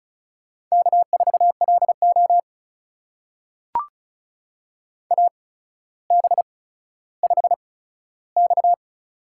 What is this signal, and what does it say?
Morse code, heard as a keyed tone, about 34 wpm, spaced as for 15 wpm, 700 Hz. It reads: K4LO AB5X
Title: USA Call Signs - 8x Character Spacing - 35wpm